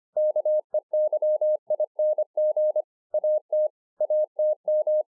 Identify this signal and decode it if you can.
Morse code, heard as a keyed tone, about 25 wpm, 610 Hz. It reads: KEYING AT ATM